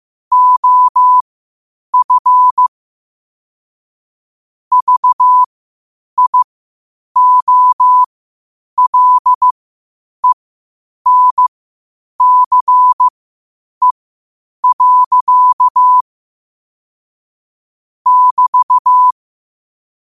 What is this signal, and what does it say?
Morse code, heard as a keyed tone, about 15 wpm, spaced as for 8 wpm, 1000 Hz. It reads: OF VIOLENCE. <BT>
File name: qotdfiles / 8wpm